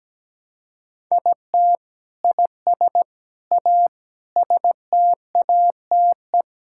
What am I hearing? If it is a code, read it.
Morse code, heard as a keyed tone, about 17 wpm, 700 Hz. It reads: IT IS A STATE
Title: Sentences from Top 1000 Words - Review Part 5 - 17wpm